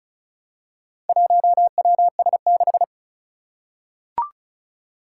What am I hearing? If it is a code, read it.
Morse code, heard as a keyed tone, about 36 wpm, 700 Hz. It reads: 1WS6
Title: Four Challenging Letter-Number Combinations 35wpm